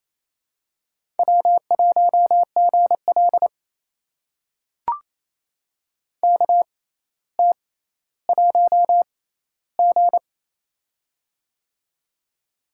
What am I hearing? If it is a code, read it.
Morse code, heard as a keyed tone, about 28 wpm, spaced as for 14 wpm, 700 Hz. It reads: W1GL KT1G
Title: USA Call Signs - 4 Characters - 6x Character Spacing - 28wpm